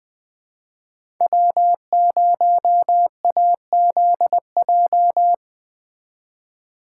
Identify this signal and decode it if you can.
Morse code, heard as a keyed tone, about 19 wpm, 700 Hz. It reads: W0AZJ